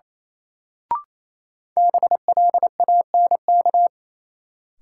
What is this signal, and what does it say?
Morse code, heard as a keyed tone, about 28 wpm, 700 Hz. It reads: BLANK
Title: Sentences from Top 1500 Words - New Word Drill - 28wpm